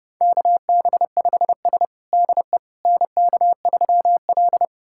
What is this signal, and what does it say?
Morse code, heard as a keyed tone, about 30 wpm, 700 Hz. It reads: KB5S DE NK3L
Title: USA Callsign DE Callsign - 4 Characters - 30wpm